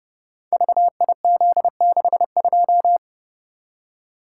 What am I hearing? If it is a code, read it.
Morse code, heard as a keyed tone, about 30 wpm, 700 Hz. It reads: VIZ62
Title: Code Groups - 5 Letter:Numbers - 30wpm